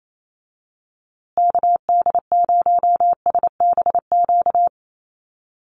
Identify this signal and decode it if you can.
Morse code, heard as a keyed tone, about 28 wpm, 700 Hz. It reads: KD0SBQ
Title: USA Call Signs Repeated 3x 28wpm